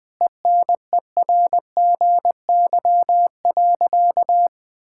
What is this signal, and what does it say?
Morse code, heard as a keyed tone, about 19 wpm, 700 Hz. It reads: ENERGY.
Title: Sentences from Top 1000 Words - Review Part 6 - 20wpm